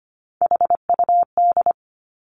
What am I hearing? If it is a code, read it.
Morse code, heard as a keyed tone, about 25 wpm, 700 Hz. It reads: HUD
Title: Three Challenging Letter-Number Combinations 25wpm